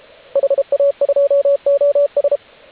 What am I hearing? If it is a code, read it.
Morse code, heard as a keyed tone, about 33 wpm, 560 Hz, with radio noise. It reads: HA2OS
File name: AUDIO2019 / HG8DX CW